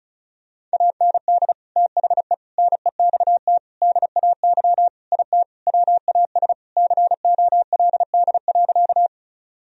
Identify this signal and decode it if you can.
Morse code, heard as a keyed tone, about 34 wpm, 700 Hz. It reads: AND THE NEXT DAY IT WAS COLD.
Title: Sentences from Top 1000 Words - Review Part 1 - 35wpm